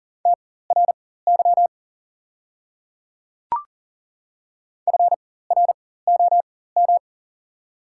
Morse code, heard as a keyed tone, about 40 wpm, spaced as for 19 wpm, 700 Hz. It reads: TRY FROM
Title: 4x Character Spacing - Top 300 Words - 40wpm